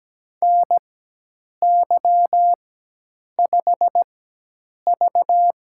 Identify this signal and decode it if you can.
Morse code, heard as a keyed tone, about 17 wpm, spaced as for 8 wpm, 700 Hz. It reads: NY5V
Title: USA Call Signs - 4x Character Spacing - 17wpm